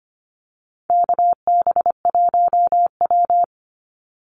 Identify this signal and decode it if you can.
Morse code, heard as a keyed tone, about 25 wpm, 700 Hz. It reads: KB1W
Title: Call Signs Repeated 3x - 25wpm